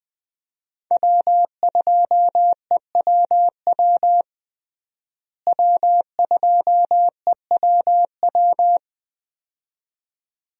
Morse code, heard as a keyed tone, about 20 wpm, 700 Hz. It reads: W2EWW W2EWW
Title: USA Call Signs - 5 Characters - Repeated 2x - 20wpm